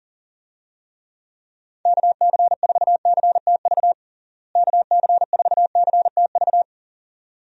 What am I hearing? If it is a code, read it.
Morse code, heard as a keyed tone, about 40 wpm, 700 Hz. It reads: KC4CTV KC4CTV
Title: USA Call Signs - 6 Characters - Repeated 2x - 40wpm